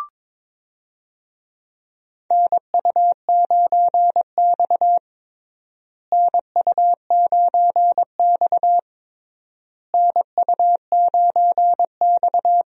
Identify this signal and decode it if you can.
Morse code, heard as a keyed tone, about 22 wpm, 700 Hz. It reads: NU9X NU9X NU9X
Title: USA Call Signs - 4 Characters - Repeated 3x - 22wpm